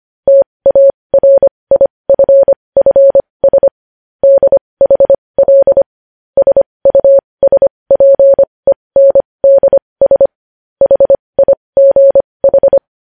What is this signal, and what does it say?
Morse code, heard as a keyed tone, about 25 wpm, spaced as for 20 wpm, 550 Hz. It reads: TARIFFS DHL SUSPENDS HIGH